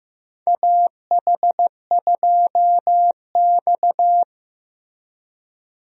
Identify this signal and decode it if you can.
Morse code, heard as a keyed tone, about 15 wpm, 700 Hz. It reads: AH2X
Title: USA Call Signs - 4 Characters - Repeated 2x - 15wpm